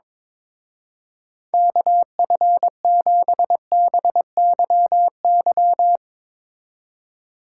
Morse code, heard as a keyed tone, about 22 wpm, 700 Hz. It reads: KF7BYY
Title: Speed Racing - USA 2x3 Call Signs 22wpm